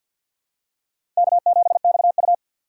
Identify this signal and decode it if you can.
Morse code, heard as a keyed tone, about 47 wpm, 700 Hz. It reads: K7XU
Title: USA Call Signs - 6x Character Spacing - 50wpm